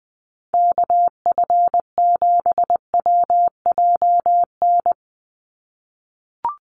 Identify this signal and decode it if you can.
Morse code, heard as a keyed tone, about 20 wpm, 700 Hz. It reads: KF7WJN
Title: USA Call Signs 20wpm